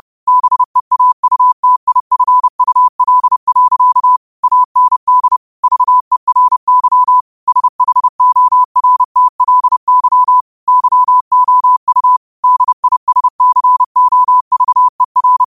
Morse code, heard as a keyed tone, about 30 wpm, 1000 Hz. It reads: BEAUTIFUL. AND VERY SHORTLY YOU DISCOVER
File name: qotdfiles / 30wpm